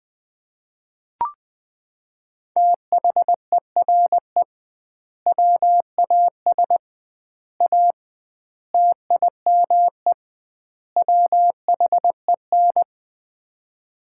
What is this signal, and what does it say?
Morse code, heard as a keyed tone, about 20 wpm, 700 Hz. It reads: THERE WAS A TIME WHEN
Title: 2x Word-Spacing - Sets of 5 Words - 20wpm